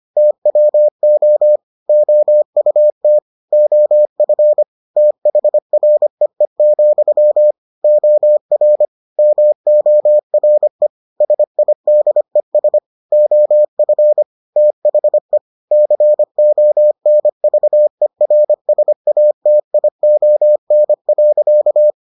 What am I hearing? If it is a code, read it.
Morse code, heard as a keyed tone, about 25 wpm, 600 Hz. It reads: TWO OUT OF THREE, OR MORE SIDES OF THE CONVERSATION.